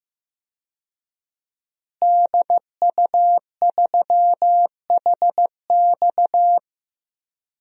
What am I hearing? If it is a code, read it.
Morse code, heard as a keyed tone, about 15 wpm, 700 Hz. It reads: DU3HX
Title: DXCC-100 Call Signs - 5 Characters - Repeated 2x - 15wpm